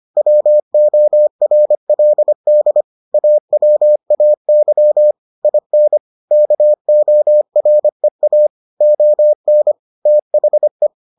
Morse code, heard as a keyed tone, about 25 wpm, 600 Hz. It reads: WORLD AWAY IN KOREA ON THE